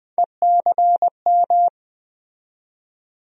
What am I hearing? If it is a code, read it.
Morse code, heard as a keyed tone, about 20 wpm, 700 Hz. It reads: ECM
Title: USA Call Sign Postfixes - 3 Letters 20wpm